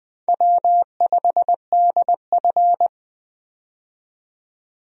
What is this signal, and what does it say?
Morse code, heard as a keyed tone, about 20 wpm, 700 Hz. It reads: W5DF
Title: USA Call Signs - 4 Characters - 20wpm